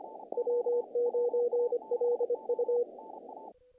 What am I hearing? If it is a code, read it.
Morse code, heard as a keyed tone, about 25 wpm, 470 Hz. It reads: W9LU
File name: contests / naqp-cw-k5tr-op-k5pi- -1800